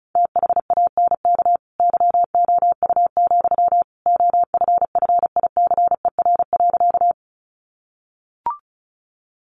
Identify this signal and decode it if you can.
Morse code, heard as a keyed tone, about 34 wpm, 700 Hz. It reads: THANK YOU, OFFICER.
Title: Sentences from Top 1200 Words 35wpm